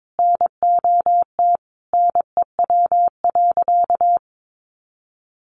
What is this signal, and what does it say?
Morse code, heard as a keyed tone, about 22 wpm, 700 Hz. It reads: NOT NEW.
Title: Sentences from Top 1100 Words 22wpm